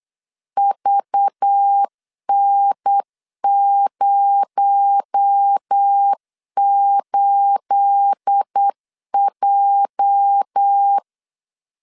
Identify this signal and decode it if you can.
Morse code, heard as a keyed tone, about 8 wpm, 800 Hz. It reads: VN08J
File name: CW / DET / 1 9 1500C S B 8wpm 5624